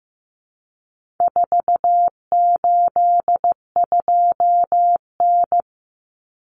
Morse code, heard as a keyed tone, about 15 wpm, 700 Hz. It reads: 482N
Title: Common Cut Numbers - 4-Digit 15wpm